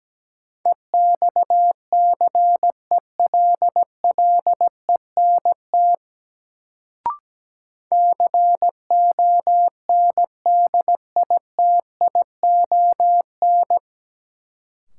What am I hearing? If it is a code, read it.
Morse code, heard as a keyed tone, about 17 wpm, 700 Hz. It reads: EXCELLENT CONDITION
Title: Sentences from Top 1000 Words - New Word Drill - 17wpm